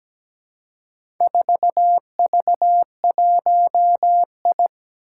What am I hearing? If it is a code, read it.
Morse code, heard as a keyed tone, about 17 wpm, 700 Hz. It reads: 4V1I
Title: Four Challenging Letter-Number Combinations 17wpm